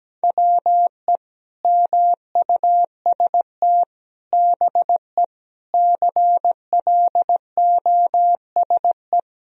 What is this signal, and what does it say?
Morse code, heard as a keyed tone, about 17 wpm, 700 Hz. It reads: WE MUST BE CLOSE
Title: Sentences from Top 1000 Words - Review Part 1 - 17wpm